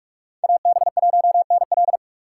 Morse code, heard as a keyed tone, about 46 wpm, 700 Hz. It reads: AB1NL
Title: USA Call Signs - 5 Characters - 3x Character Spacing - 45wpm